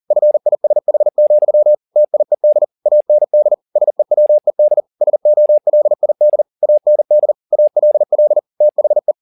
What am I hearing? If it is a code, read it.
Morse code, heard as a keyed tone, about 40 wpm, 600 Hz. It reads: FISH, TIED AND SEWED SOLID AND ALL THE